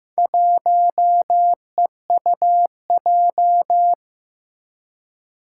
Morse code, heard as a keyed tone, about 15 wpm, 700 Hz. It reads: 1EUJ
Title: Four Challenging Letter-Number Combinations 15wpm